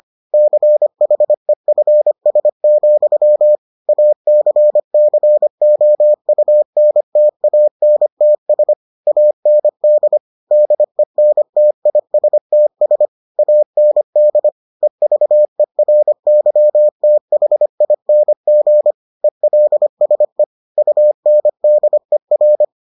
Morse code, heard as a keyed tone, about 25 wpm, 600 Hz. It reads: CHEFS, ACCOUNTANTS AND DENTISTS AND EVERYTHING ELSE UNDER